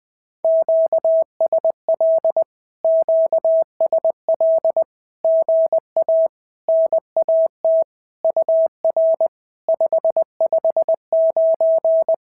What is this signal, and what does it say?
Morse code, heard as a keyed tone, about 20 wpm, 650 Hz. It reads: QSL QSL GA NAT UR 559
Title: SOTA QSO Practice 20wpm